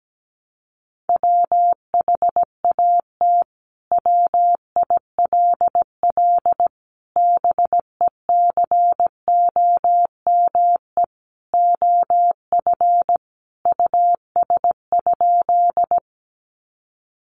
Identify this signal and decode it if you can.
Morse code, heard as a keyed tone, about 17 wpm, 700 Hz. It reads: WHAT WILL BECOME OF US?
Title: Sentences from Top 600 Words 17wpm